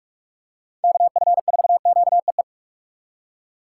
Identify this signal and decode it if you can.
Morse code, heard as a keyed tone, about 45 wpm, 700 Hz. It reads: KF4QEE
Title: USA Call Signs - 6x Character Spacing - 45wpm